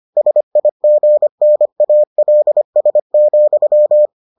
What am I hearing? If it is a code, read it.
Morse code, heard as a keyed tone, about 25 wpm, 600 Hz. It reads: SIGNALS,